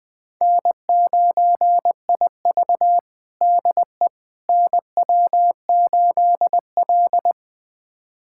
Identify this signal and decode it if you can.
Morse code, heard as a keyed tone, about 20 wpm, 700 Hz. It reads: N9IV DE NW8L
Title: USA Callsign DE Callsign - 4 Characters - 20wpm